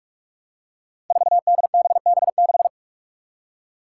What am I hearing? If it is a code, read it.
Morse code, heard as a keyed tone, about 44 wpm, 700 Hz. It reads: 4DBB6
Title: Five Challenging Letter-Number Combinations 45wpm